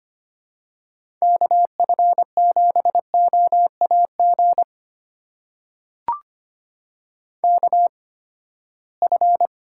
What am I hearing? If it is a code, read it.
Morse code, heard as a keyed tone, about 25 wpm, spaced as for 17 wpm, 700 Hz. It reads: KF7OAG KF
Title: USA Call Signs - 6 Characters - 8x Character Spacing - 25wpm